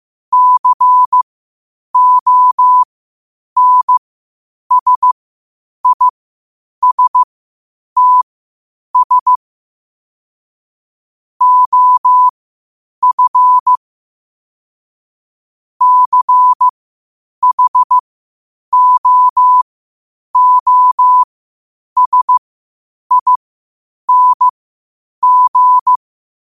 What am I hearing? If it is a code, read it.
Morse code, heard as a keyed tone, about 15 wpm, spaced as for 8 wpm, 1000 Hz. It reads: CONSISTS OF CHOOSING
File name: 8wpm